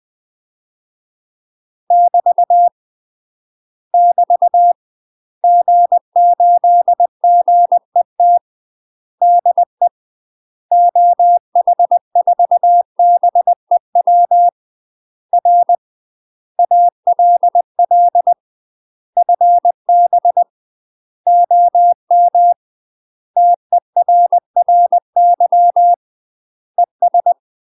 Morse code, heard as a keyed tone, about 20 wpm, 700 Hz. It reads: <BT> <BT> G8GET DE OH4BEW R ALL FB OM TERRY ES
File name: UK MASTER QSOs TRAINING/20WPM